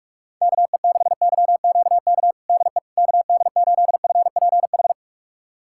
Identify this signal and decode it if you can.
Morse code, heard as a keyed tone, about 41 wpm, 700 Hz. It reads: KE6YQK DE KD8FPH
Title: USA Callsign DE Callsign 45wpm